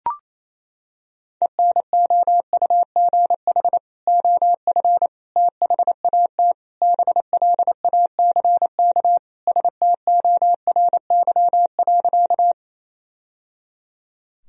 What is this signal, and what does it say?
Morse code, heard as a keyed tone, about 28 wpm, 700 Hz. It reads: ENOUGH OF THAT BLACK STORY.
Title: Sentences from Top 1000 Words - Review Part 2 - 28wpm